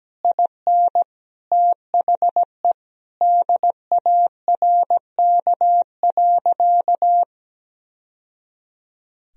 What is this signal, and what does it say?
Morse code, heard as a keyed tone, about 17 wpm, 700 Hz. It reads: IN THE DARK.
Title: Sentences from Top 1000 Words - Review Part 2 - 17wpm